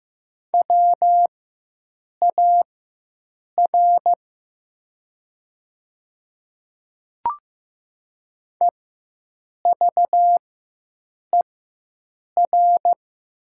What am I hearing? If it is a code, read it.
Morse code, heard as a keyed tone, about 15 wpm, spaced as for 7 wpm, 700 Hz. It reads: WAR EVER
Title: 4x Character Spacing - Top 500 Words - 15wpm